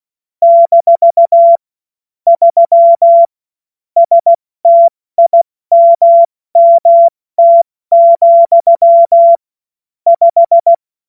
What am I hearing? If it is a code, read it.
Morse code, heard as a keyed tone, about 16 wpm, spaced as for 14 wpm, 680 Hz. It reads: - 3 STIMMT, 5
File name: Morsekurs / Witze / Witz 12. April 2022 70 F80